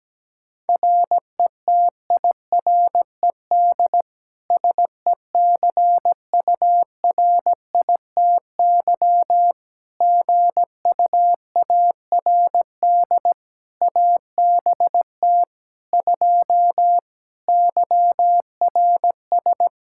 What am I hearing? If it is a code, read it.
Morse code, heard as a keyed tone, about 17 wpm, 700 Hz. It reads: RETIRED SECURITY GUARD ABT 2 YRS
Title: Practice QSO - Exchange 3 - More Info - 17wpm